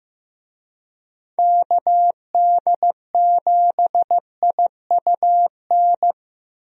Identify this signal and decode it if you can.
Morse code, heard as a keyed tone, about 15 wpm, 700 Hz. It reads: KD7IUN
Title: USA Call Signs - 6 Characters - Repeated 3x - 15wpm